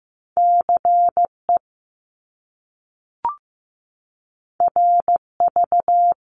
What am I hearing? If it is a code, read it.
Morse code, heard as a keyed tone, about 15 wpm, 700 Hz. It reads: CE RV
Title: Top Two Letter Combinations in Words 15wpm